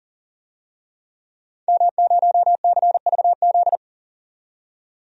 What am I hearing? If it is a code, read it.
Morse code, heard as a keyed tone, about 40 wpm, 700 Hz. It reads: M0CVZ
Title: DXCC-100 Call Signs - 5 Characters - 40wpm